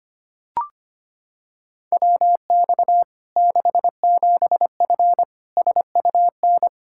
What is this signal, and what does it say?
Morse code, heard as a keyed tone, about 25 wpm, 700 Hz. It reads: WX 67F SUN
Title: K1USN SST QSO Practice 25wpm